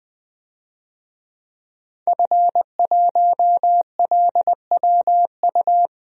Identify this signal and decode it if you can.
Morse code, heard as a keyed tone, about 20 wpm, 700 Hz. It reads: F1LWU
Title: DXCC-100 Call Signs - 5 Characters - 20wpm